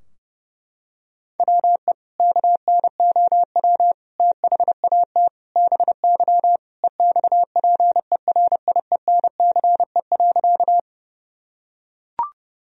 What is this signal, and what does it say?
Morse code, heard as a keyed tone, about 32 wpm, 700 Hz. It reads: WE KNOW THAT BY EXPERIENCE.
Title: Sentences from Top 1000 Words - Review Part 3 - 30wpm